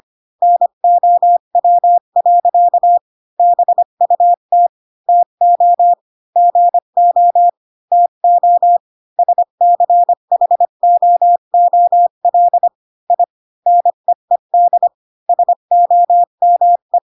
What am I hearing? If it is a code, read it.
Morse code, heard as a keyed tone, about 25 wpm, 700 Hz. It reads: NOW. BUT TO GO TO SCHOOL I NEED SOME
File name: Pinocchio / Pinocchio ch8 20 25 ik1hge